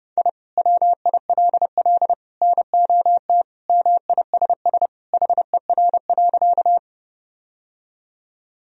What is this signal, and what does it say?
Morse code, heard as a keyed tone, about 32 wpm, 700 Hz. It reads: I WILL NOT MISS HER.
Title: Sentences from Top 1000 Words - Review Part 3 - 30wpm